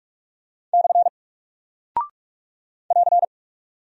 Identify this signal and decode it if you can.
Morse code, heard as a keyed tone, about 44 wpm, 700 Hz. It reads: / <AR>
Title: Single Character (Letter:Number:Punctuation:Prosign) Alt 45wpm